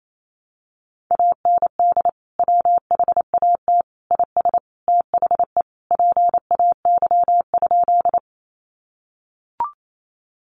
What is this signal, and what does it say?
Morse code, heard as a keyed tone, about 28 wpm, 700 Hz. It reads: AND WHAT IS THE PAY?
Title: Sentences from Top 700 Words 28wpm